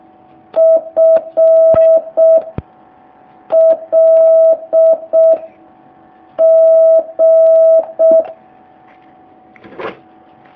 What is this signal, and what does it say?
Morse code, heard as a keyed tone, about 6 wpm, 650 Hz. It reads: F L G
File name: jh8cbh / 2019cw